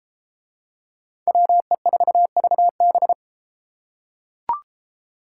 Morse code, heard as a keyed tone, about 31 wpm, 700 Hz. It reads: WE4VB